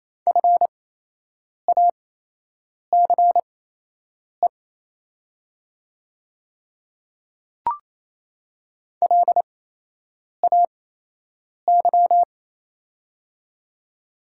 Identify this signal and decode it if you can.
Morse code, heard as a keyed tone, about 28 wpm, spaced as for 8 wpm, 700 Hz. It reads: FACE LAY
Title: 8x Character Spacing - Top 500 Words - 28wpm